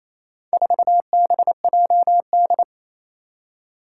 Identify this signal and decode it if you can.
Morse code, heard as a keyed tone, about 29 wpm, 700 Hz. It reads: 4BJD
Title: Four Challenging Letter-Number Combinations 28wpm